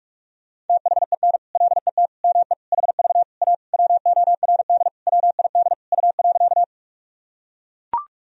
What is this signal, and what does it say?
Morse code, heard as a keyed tone, about 42 wpm, 700 Hz. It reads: THEN LET ME HV A WORD WID U.
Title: Sentences from Top 300 Words and Abbreviations 45wpm